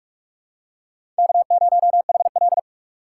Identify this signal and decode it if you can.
Morse code, heard as a keyed tone, about 41 wpm, 700 Hz. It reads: K0HL
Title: USA Call Signs - 4 Characters - 3x Character Spacing - 45wpm